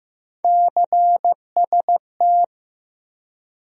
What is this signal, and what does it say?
Morse code, heard as a keyed tone, about 15 wpm, 700 Hz. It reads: CST
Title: USA Call Sign Postfixes - 3 Letters 15wpm